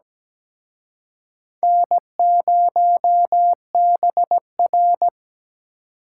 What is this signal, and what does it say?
Morse code, heard as a keyed tone, about 20 wpm, 700 Hz. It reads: N0BR
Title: Speed Racing - USA 1x2 Call Signs 17wpm